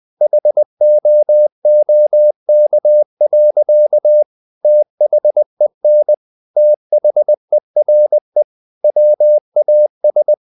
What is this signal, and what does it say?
Morse code, heard as a keyed tone, about 20 wpm, 600 Hz. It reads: HOOK. THEN THERE WAS
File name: the old man and the sea / the old man and the sea part 1 20wpm